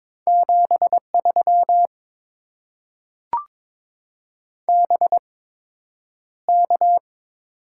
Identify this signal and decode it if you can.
Morse code, heard as a keyed tone, about 22 wpm, spaced as for 9 wpm, 700 Hz. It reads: 73 BK
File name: Top 90 QSO Elements - 8x Character Spacing - 22wpm